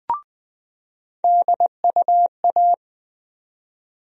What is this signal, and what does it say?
Morse code, heard as a keyed tone, about 20 wpm, 700 Hz. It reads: DUA
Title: USA Call Sign Postfixes - 3 Letters 20wpm